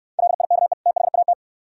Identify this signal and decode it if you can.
Morse code, heard as a keyed tone, about 69 wpm, 700 Hz. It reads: LIKE THAT